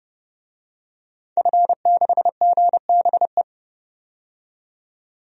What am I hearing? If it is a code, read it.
Morse code, heard as a keyed tone, about 30 wpm, 700 Hz. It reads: F6GBE
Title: DXCC-100 Call Signs - 5 Characters - 30wpm